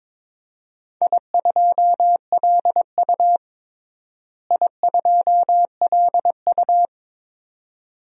Morse code, heard as a keyed tone, about 22 wpm, 700 Hz. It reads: I2LU I2LU
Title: DXCC-100 Call Signs - 4 Characters - Repeated 2x - 22wpm